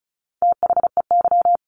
Morse code, heard as a keyed tone, about 34 wpm, 700 Hz. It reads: THEY
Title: Additional Easy Phrases 35wpm